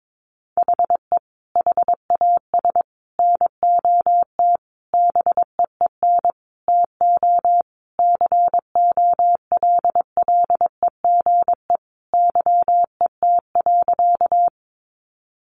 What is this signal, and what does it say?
Morse code, heard as a keyed tone, about 22 wpm, 700 Hz. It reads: HE HAS NOT BEEN TO COLLEGE YET.
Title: Sentences from Top 1300 Words 22wpm